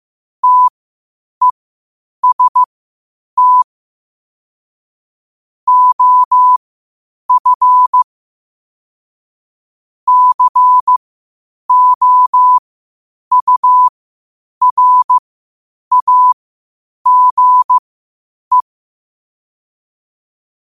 Morse code, heard as a keyed tone, about 15 wpm, spaced as for 8 wpm, 1000 Hz. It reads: TEST OF COURAGE
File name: qotdfiles / 8wpm